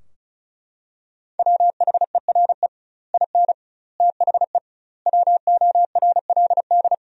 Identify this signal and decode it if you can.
Morse code, heard as a keyed tone, about 36 wpm, 700 Hz. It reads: WHERE IN THE WORLD
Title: 2x Word-Spacing - Sentences from Top 200 Words - 35wpm